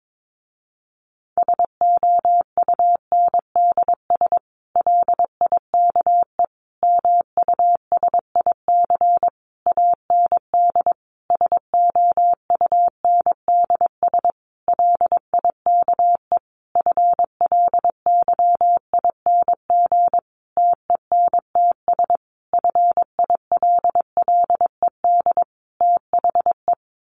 Morse code, heard as a keyed tone, about 22 wpm, 700 Hz. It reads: SOUNDS LIKE MUSIC AND SOUNDS LIKE FLYING TENTS FILLED THE
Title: A Sound Of Thunder 22wpm